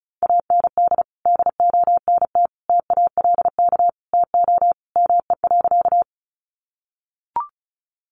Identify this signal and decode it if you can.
Morse code, heard as a keyed tone, about 34 wpm, 700 Hz. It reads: AND DONT TALK TO ME.
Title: Sentences from Top 400 Words 35wpm